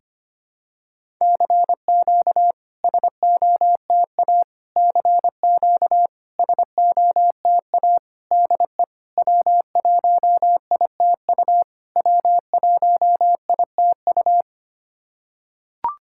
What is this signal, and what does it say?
Morse code, heard as a keyed tone, about 25 wpm, 690 Hz. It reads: CQ SOTA CQ SOTA DE W1ITU W1ITU
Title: SOTA QSO Practice 25wpm